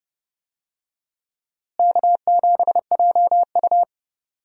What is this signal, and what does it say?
Morse code, heard as a keyed tone, about 29 wpm, 700 Hz. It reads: K7JU
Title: USA Call Signs - 4 Characters - Repeated 2x - 30wpm